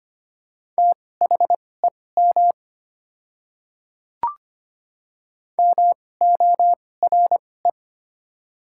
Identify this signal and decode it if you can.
Morse code, heard as a keyed tone, about 25 wpm, spaced as for 18 wpm, 690 Hz. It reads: THEM MORE
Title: 2x Character Spacing - Top 100 Words - 25wpm